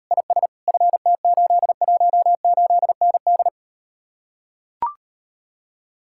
Morse code, heard as a keyed tone, about 39 wpm, 700 Hz. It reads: IS FT818ND